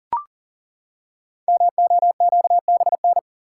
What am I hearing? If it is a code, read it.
Morse code, heard as a keyed tone, about 40 wpm, 700 Hz. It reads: MOQBN
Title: Code Groups - 5 Letters - 40wpm